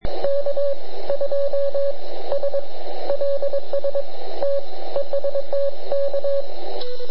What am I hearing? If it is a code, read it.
Morse code, heard as a keyed tone, about 22 wpm, spaced as for 17 wpm, 560 Hz, with radio noise. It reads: K2SLS TEST K
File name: ARCHIV / QRP2005 / ok2sls